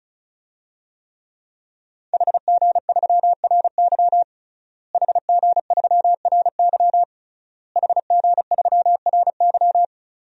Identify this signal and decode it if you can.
Morse code, heard as a keyed tone, about 36 wpm, 700 Hz. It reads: HG3RY HG3RY HG3RY
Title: DXCC-100 Call Signs - 5 Characters - Repeated 3x - 35wpm